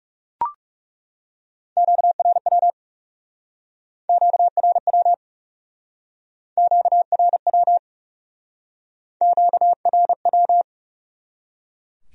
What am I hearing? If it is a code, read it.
Morse code, heard as a keyed tone, about 36 wpm, 700 Hz. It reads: QRW QRW QRW QRW
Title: Speed-Racing Q-Codes - 30wpm